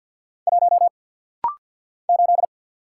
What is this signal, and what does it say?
Morse code, heard as a keyed tone, about 48 wpm, 700 Hz. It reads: 1 8
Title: Single Letter-Number - Rapid-Fire 50wpm